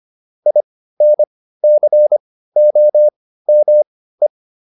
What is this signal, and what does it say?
Morse code, heard as a keyed tone, about 25 wpm, spaced as for 15 wpm, 600 Hz. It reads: INCOME